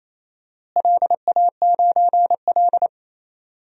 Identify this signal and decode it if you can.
Morse code, heard as a keyed tone, about 27 wpm, 700 Hz. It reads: LA9L
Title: DXCC-100 Call Signs - 4 Characters - Repeated 2x - 28wpm